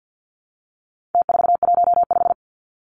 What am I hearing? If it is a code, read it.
Morse code, heard as a keyed tone, about 50 wpm, 700 Hz. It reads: T415
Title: Numbers - Contest Style - 4-Digit 50wpm